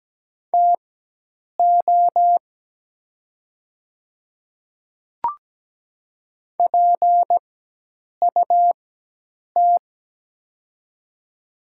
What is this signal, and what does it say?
Morse code, heard as a keyed tone, about 17 wpm, spaced as for 8 wpm, 700 Hz. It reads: TO PUT
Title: 4x Character Spacing - Top 200 Words - 17wpm